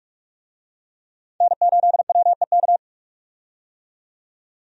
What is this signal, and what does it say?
Morse code, heard as a keyed tone, about 46 wpm, 700 Hz. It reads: N8WEK
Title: USA Call Signs - 5 Characters - 45wpm